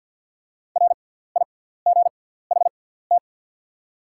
Morse code, heard as a keyed tone, about 48 wpm, spaced as for 17 wpm, 700 Hz. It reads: RIGHT
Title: 6x Character Spacing - Top 200 Words - 50wpm